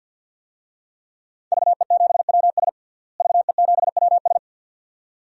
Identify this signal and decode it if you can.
Morse code, heard as a keyed tone, about 47 wpm, 700 Hz. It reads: VE7WS VE7WS
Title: DXCC-100 Call Signs - 5 Characters - Repeated 2x - 50wpm